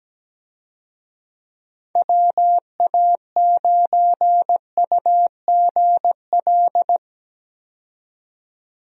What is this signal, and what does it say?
Morse code, heard as a keyed tone, about 17 wpm, 700 Hz. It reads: WA9UGL
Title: USA Call Signs - 6 Characters - 17wpm